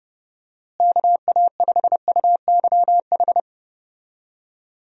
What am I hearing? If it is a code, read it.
Morse code, heard as a keyed tone, about 31 wpm, 700 Hz. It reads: KA5UYH
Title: USA Call Signs - 6 Characters - Repeated 3x - 30wpm